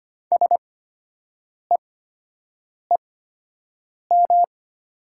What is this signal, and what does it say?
Morse code, heard as a keyed tone, about 25 wpm, spaced as for 7 wpm, 700 Hz. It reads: SEEM
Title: 8x Character Spacing - Top 500 Words - 25wpm